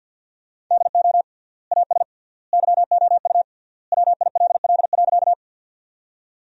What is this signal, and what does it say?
Morse code, heard as a keyed tone, about 53 wpm, 700 Hz. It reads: DO AS YOU WILL.
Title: 3x Word-Spacing - Sentences from Top 100 Words - 50wpm